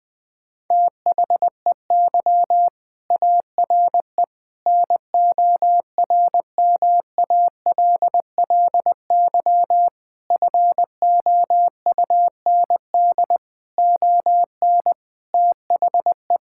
Morse code, heard as a keyed tone, about 20 wpm, 700 Hz. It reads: THEY ARE NORMALLY FOUND ON THE
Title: Sentences from Top 1000 Words - Review Part 6 - 20wpm